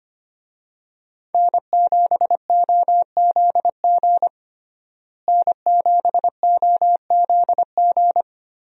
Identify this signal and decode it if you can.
Morse code, heard as a keyed tone, about 25 wpm, 700 Hz. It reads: N7OZG N7OZG
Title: USA Call Signs - 5 Characters - Repeated 3x - 25wpm